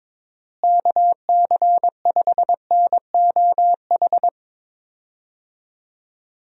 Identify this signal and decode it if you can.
Morse code, heard as a keyed tone, about 22 wpm, 700 Hz. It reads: KC5NOH
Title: USA Call Signs - 6 Characters - Repeated 3x - 22wpm